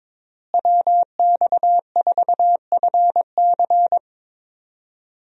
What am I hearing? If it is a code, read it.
Morse code, heard as a keyed tone, about 22 wpm, 700 Hz. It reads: WX4FC
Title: USA Call Signs - 5 Characters - Repeated 3x - 22wpm